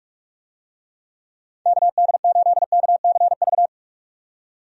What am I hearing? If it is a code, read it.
Morse code, heard as a keyed tone, about 46 wpm, 700 Hz. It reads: KD8KCV
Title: USA Call Signs - 6 Characters - 45wpm